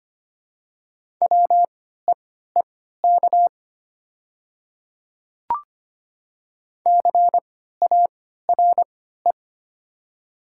Morse code, heard as a keyed tone, about 25 wpm, spaced as for 14 wpm, 700 Hz. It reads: WEEK CARE
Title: 3x Character Spacing - Top 500 Words - 25wpm